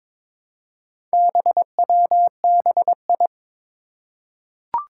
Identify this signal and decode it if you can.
Morse code, heard as a keyed tone, about 22 wpm, 700 Hz. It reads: BWBI
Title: Four Challenging Letter-Number Combinations 22wpm